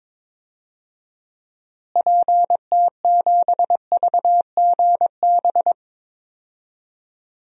Morse code, heard as a keyed tone, about 22 wpm, 700 Hz. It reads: PT7VGB
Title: DXCC-100 Call Signs - 6 Characters - Repeated 2x - 22wpm